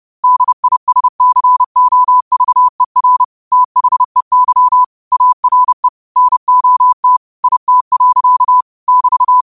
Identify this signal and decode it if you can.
Morse code, heard as a keyed tone, about 30 wpm, 1000 Hz. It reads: DISCOVER THEY ARE NOT IT. <BT>